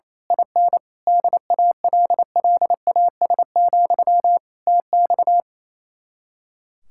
Morse code, heard as a keyed tone, about 26 wpm, 700 Hz. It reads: IN DALLAS, TX
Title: Speed-Racing Common QSO Phrases - 28wpm